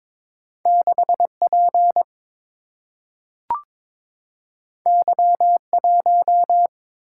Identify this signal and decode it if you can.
Morse code, heard as a keyed tone, about 22 wpm, 700 Hz. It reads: 6P Y1
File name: practice / 22wpm / Two Letter-Number Combinations - 22wpm